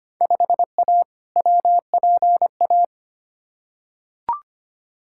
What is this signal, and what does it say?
Morse code, heard as a keyed tone, about 25 wpm, 700 Hz. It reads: 5A WPA
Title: Field Day Exchanges Encore - 25wpm